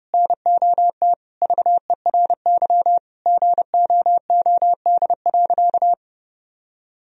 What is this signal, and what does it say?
Morse code, heard as a keyed tone, about 30 wpm, 700 Hz. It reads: NOT VERY GOOD.
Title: Sentences from Top 1000 Words - Review Part 3 - 30wpm